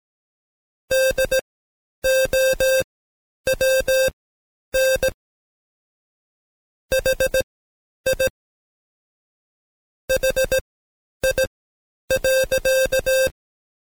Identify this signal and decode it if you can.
Morse code, heard as a keyed tone, about 17 wpm, spaced as for 9 wpm, 530 Hz. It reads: DOWN HI HI.
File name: Code Quick / AudioDownld2 / Seg131-Stepper4 9WPM